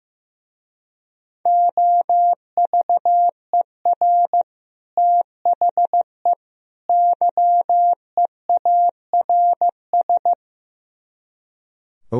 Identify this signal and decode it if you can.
Morse code, heard as a keyed tone, about 15 wpm, 700 Hz. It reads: OVER THE YEARS
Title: Speed-Racing - Sets of 3 Words 15wpm